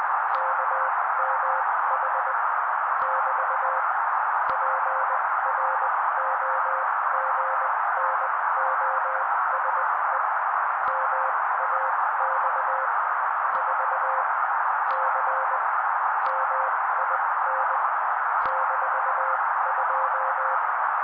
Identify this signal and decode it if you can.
Morse code, heard as a keyed tone, about 20 wpm, spaced as for 16 wpm, 530 Hz. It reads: KMH <BT> PROGNOSE MAX 4 C MIN -2
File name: daily / 16wpm noise